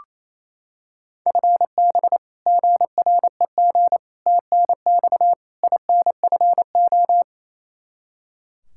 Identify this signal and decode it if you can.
Morse code, heard as a keyed tone, about 27 wpm, 700 Hz. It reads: FB GREG TNX INFO
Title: Practice QSO - Exchange 3 - More Info - 28wpm